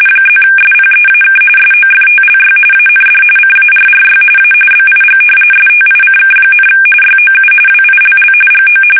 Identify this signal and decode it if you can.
RTTY, about 45.45 baud. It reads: +SPEAK SOFTLY AND CARRY A BIG STICK, YOU WILL GO
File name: fsk850 / usnavy3